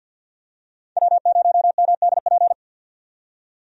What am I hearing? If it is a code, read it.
Morse code, heard as a keyed tone, about 50 wpm, 700 Hz. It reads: W0MDP
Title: USA Call Signs - 5 Characters - 50wpm